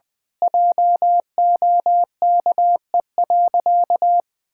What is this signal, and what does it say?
Morse code, heard as a keyed tone, about 20 wpm, 700 Hz. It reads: JOKE.
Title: Pangram Challenge v2 - 20wpm